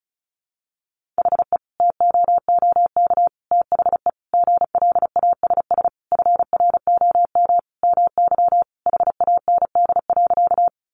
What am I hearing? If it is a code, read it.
Morse code, heard as a keyed tone, about 34 wpm, 700 Hz. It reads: HE TOOK THE GLASS FROM MY HAND.
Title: Sentences from Top 1300 Words 35wpm